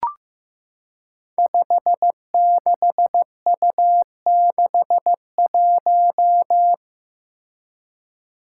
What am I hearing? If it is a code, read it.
Morse code, heard as a keyed tone, about 15 wpm, 700 Hz. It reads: 56U61
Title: Five Challenging Letter-Number Combinations 15wpm